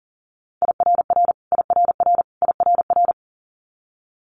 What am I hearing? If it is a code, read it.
Morse code, heard as a keyed tone, about 40 wpm, 700 Hz. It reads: IRR IRR IRR
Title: Multi-letter Phonemes Repeated 3x 40wpm